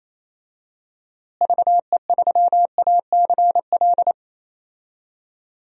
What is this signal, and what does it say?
Morse code, heard as a keyed tone, about 28 wpm, 700 Hz. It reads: VE3ACL
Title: DXCC-100 Call Signs - 6 Characters - Repeated 3x - 28wpm